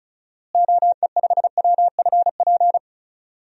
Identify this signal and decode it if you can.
Morse code, heard as a keyed tone, about 36 wpm, 700 Hz. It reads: OE5WFP
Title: DXCC-100 Call Signs - 6 Characters - Repeated 2x - 35wpm